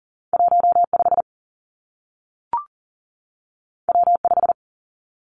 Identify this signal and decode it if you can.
Morse code, heard as a keyed tone, about 39 wpm, 700 Hz. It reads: 1 5 W 5
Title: Two Challenging Letter-Number Combinations 40wpm